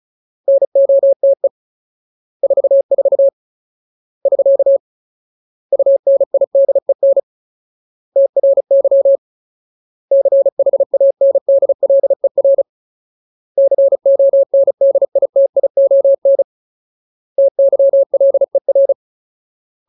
Morse code, heard as a keyed tone, about 34 wpm, 550 Hz. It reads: NOTE 44 <SK> UNIDEN TRY CHANDLER CONDITION TYLER
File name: Ultimate Word List / 35WPM Ultimate